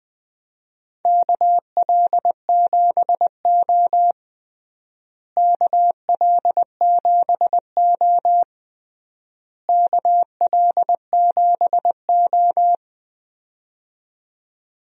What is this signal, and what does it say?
Morse code, heard as a keyed tone, about 20 wpm, 700 Hz. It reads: KL7O KL7O KL7O
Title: DXCC-100 Call Signs - 4 Characters - Repeated 3x - 20wpm